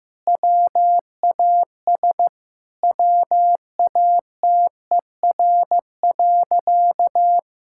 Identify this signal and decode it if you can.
Morse code, heard as a keyed tone, about 15 wpm, 700 Hz. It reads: WAS WATER.
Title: Sentences from Top 1600 Words 15wpm